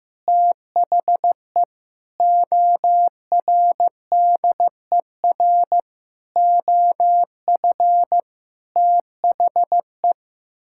Morse code, heard as a keyed tone, about 16 wpm, 700 Hz. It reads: THE ORDER OF THE
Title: Sentences from Top 1000 Words - Review Part 5 - 15wpm